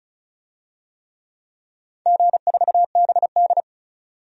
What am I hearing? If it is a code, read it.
Morse code, heard as a keyed tone, about 36 wpm, 700 Hz. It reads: G4BD
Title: DXCC-100 Call Signs - 4 Characters - 35wpm